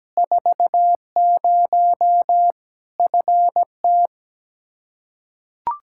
Morse code, heard as a keyed tone, about 17 wpm, 700 Hz. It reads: 40 FT
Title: Practice QSO - Exchange 2 - Info - 17wpm